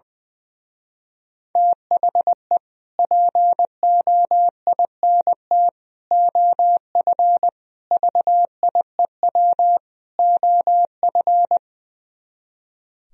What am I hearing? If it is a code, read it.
Morse code, heard as a keyed tone, about 22 wpm, 700 Hz. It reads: THE POINT OF VIEW OF
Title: Speed-Racing - Sets of 5 Words 20wpm